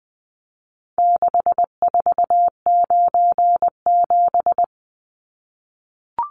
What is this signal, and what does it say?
Morse code, heard as a keyed tone, about 20 wpm, 700 Hz. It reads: 6497
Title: Numbers - 4-Digit 20wpm